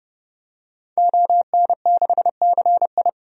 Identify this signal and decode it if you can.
Morse code, heard as a keyed tone, about 30 wpm, 700 Hz. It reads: ON6CI
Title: DXCC-100 Call Signs - 5 Characters - 30wpm